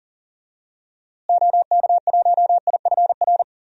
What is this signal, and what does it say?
Morse code, heard as a keyed tone, about 37 wpm, 700 Hz. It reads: OK1IFR
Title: DXCC-100 Call Signs - 6 Characters - Repeated 3x - 40wpm